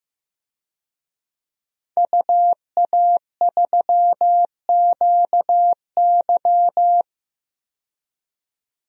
Morse code, heard as a keyed tone, about 15 wpm, 700 Hz. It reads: UA3QY
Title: DXCC-100 Call Signs - 5 Characters - 15wpm